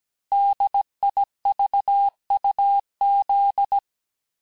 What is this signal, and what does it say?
Morse code, heard as a keyed tone, about 17 wpm, 800 Hz. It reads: DIVUZ
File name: corrige-cours-1581-texte-chiffre-vitesse-16-mpm